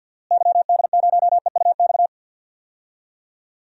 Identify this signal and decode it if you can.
Morse code, heard as a keyed tone, about 52 wpm, 700 Hz. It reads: YD0EUX
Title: DXCC-100 Call Signs - 6 Characters - Repeated 3x - 50wpm